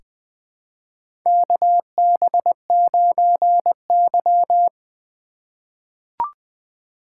Morse code, heard as a keyed tone, about 20 wpm, 700 Hz. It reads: KB9Y
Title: USA Call Signs - 4 Characters - 2x Character Spacing - 20wpm